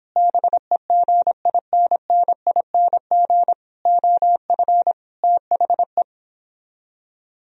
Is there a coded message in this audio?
Yes: Morse code from 0.2 to 6.0 s, heard as a keyed tone, about 31 wpm, 700 Hz. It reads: BEGINNING OF THE